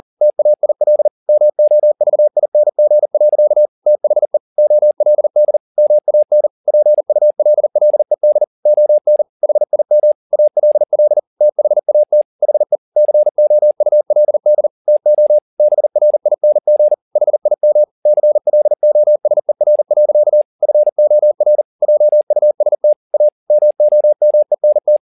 Morse code, heard as a keyed tone, about 40 wpm, 600 Hz. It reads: TAIL MOVING. THE OLD MAN PULLED ON HIM ALL THAT HE COULD TO BRING HIM CLOSER. FOR JUST A MOMENT
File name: the old man and the sea 2 40wpm